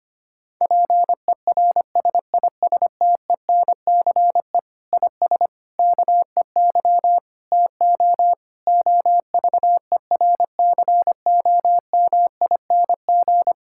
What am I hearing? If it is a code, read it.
Morse code, heard as a keyed tone, about 25 wpm, 700 Hz. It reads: PERSISTENCE IS KEY TO OVERCOMING
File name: practice / 25wpm / Difficult Sentences with Hints - 25wpm